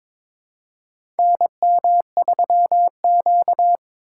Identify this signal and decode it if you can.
Morse code, heard as a keyed tone, about 22 wpm, 700 Hz. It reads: NM3Q
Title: USA Call Signs - 2x Character Spacing - 22wpm